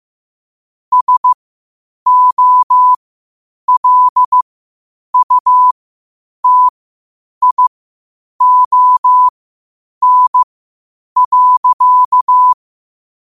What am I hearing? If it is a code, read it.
Morse code, heard as a keyed tone, about 15 wpm, spaced as for 8 wpm, 1000 Hz. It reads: SOLUTION.